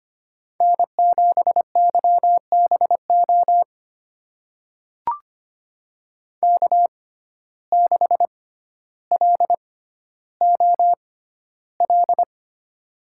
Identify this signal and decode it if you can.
Morse code, heard as a keyed tone, about 25 wpm, spaced as for 13 wpm, 700 Hz. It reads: N7YBO K6LOL
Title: USA Call Signs - 5 Characters - 6x Character Spacing - 25wpm